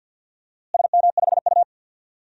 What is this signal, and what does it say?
Morse code, heard as a keyed tone, about 53 wpm, 700 Hz. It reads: SM5U
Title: DXCC-100 Call Signs - 4 Characters - Repeated 3x - 50wpm